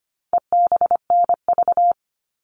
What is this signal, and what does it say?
Morse code, heard as a keyed tone, about 25 wpm, 700 Hz. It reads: EBNV
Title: All Cut Numbers - 4-Digit 25wpm